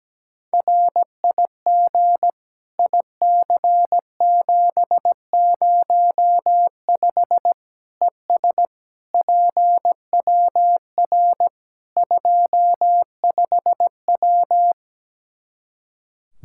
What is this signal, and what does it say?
Morse code, heard as a keyed tone, about 17 wpm, 700 Hz. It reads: RIG IC705 ES PWR 25W
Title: Practice QSO - Exchange 2 - Info - 17wpm